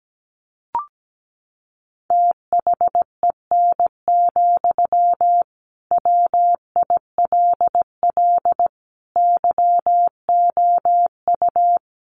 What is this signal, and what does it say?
Morse code, heard as a keyed tone, about 17 wpm, 700 Hz. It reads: THEN, WILL YOU
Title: Sentences from Top 1400 Words 17wpm